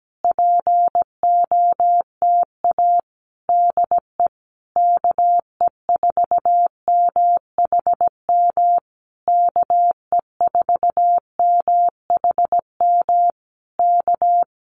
Morse code, heard as a keyed tone, about 17 wpm, 700 Hz. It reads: POTA DE KE4MHM KE4MHM K
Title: Common QSO Phrases Encore 17wpm